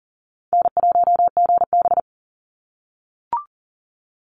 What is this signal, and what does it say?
Morse code, heard as a keyed tone, about 40 wpm, 700 Hz. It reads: N1GB
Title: NA SOTA Chaser Callsigns 40wpm